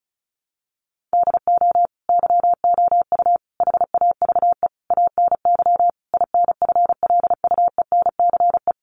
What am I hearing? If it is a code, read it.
Morse code, heard as a keyed tone, about 34 wpm, 700 Hz. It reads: DO YOU HAVE ANY INFLUENCE
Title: Sentences from Top 1100 Words 35wpm